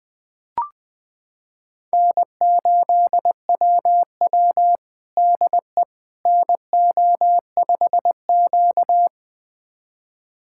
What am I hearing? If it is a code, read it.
Morse code, heard as a keyed tone, about 20 wpm, 700 Hz. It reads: N8WW DE NO5Q
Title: USA Callsign DE Callsign - 4 Characters - 20wpm